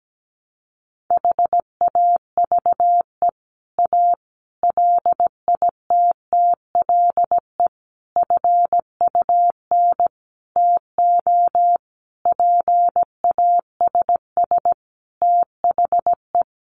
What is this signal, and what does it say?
Morse code, heard as a keyed tone, about 17 wpm, 700 Hz. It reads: HAVE A LITTLE FUN TO PASS THE
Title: Sentences from Top 700 Words 17wpm